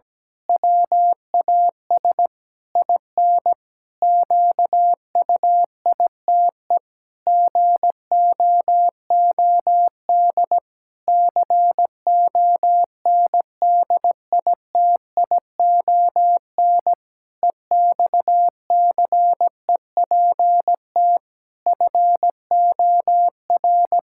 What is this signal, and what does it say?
Morse code, heard as a keyed tone, about 17 wpm, 700 Hz. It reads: WAS IN QUITE GOOD CONDITION EXCEPT FOR
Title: Pangram Challenge v2 - 17wpm